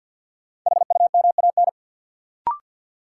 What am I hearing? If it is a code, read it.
Morse code, heard as a keyed tone, about 52 wpm, 700 Hz. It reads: HUMAN
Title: Sentences from Top 600 Words - New Word Drill - 50wpm